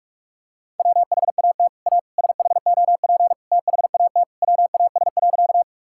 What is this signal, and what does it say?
Morse code, heard as a keyed tone, about 41 wpm, 700 Hz. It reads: WHAT A SHOP THAT WAS.
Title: Sentences from Top 1600 Words 45wpm